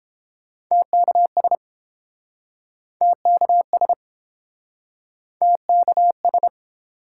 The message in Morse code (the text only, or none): TKS TKS TKS